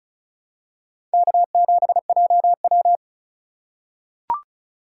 Morse code, heard as a keyed tone, about 34 wpm, 700 Hz. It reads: K7JW
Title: USA Call Signs - 2x Character Spacing - 35wpm